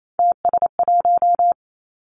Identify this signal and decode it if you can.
Morse code, heard as a keyed tone, about 28 wpm, 700 Hz. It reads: TS1
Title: Three Challenging Letter-Number Combinations 28wpm